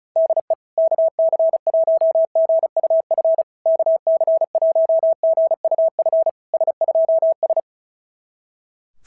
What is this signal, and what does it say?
Morse code, heard as a keyed tone, about 36 wpm, 650 Hz. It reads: DE KC1GUF KC1GUF S2S
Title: SOTA QSO Practice 35wpm